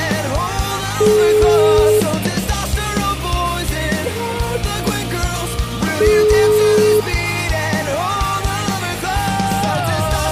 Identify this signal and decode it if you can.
Ringback tone (Europe, 425 Hz), with noise.